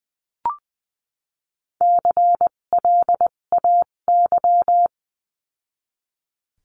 Morse code, heard as a keyed tone, about 20 wpm, spaced as for 17 wpm, 700 Hz. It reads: CLAY
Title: 257 Lesson 42b - Words 14-20wpm